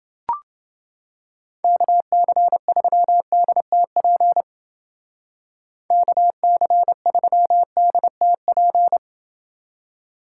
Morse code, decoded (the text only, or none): KC3DTP KC3DTP